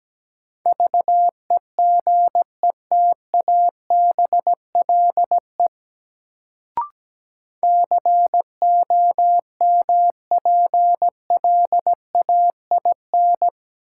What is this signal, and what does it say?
Morse code, heard as a keyed tone, about 17 wpm, 700 Hz. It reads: VEGETABLE COMPLAIN
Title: Sentences from Top 1300 Words - New Word Drill - 17wpm